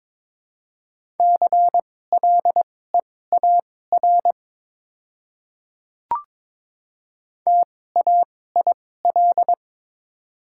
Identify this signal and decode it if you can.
Morse code, heard as a keyed tone, about 22 wpm, spaced as for 16 wpm, 700 Hz. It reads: CLEAR TAIL
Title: 2x Character Spacing - Top 500 Words - 22wpm